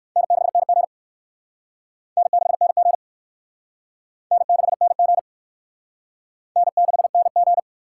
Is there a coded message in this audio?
Yes: Morse code from 0.2 to 7.6 s, heard as a keyed tone, about 57 wpm, 700 Hz. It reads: N6NG N6NG N6NG N6NG